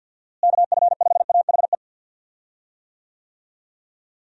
Morse code, heard as a keyed tone, about 50 wpm, 700 Hz. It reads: KF5AHE
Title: USA Call Signs - 6 Characters - Repeated 2x - 50wpm